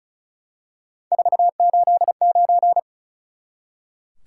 Morse code, heard as a keyed tone, about 36 wpm, 700 Hz. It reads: 489
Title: Top 180 QSO Elements 35wpm